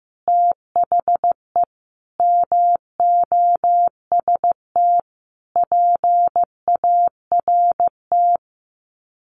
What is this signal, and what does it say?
Morse code, heard as a keyed tone, about 15 wpm, 700 Hz. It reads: THE MOST PART
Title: Sets of 5 Words Encore 15wpm